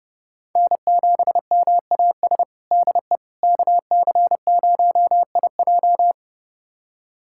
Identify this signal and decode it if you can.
Morse code, heard as a keyed tone, about 30 wpm, 700 Hz. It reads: N7MAS DE KC0IJ
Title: USA Callsign DE Callsign - 5 Characters - 30wpm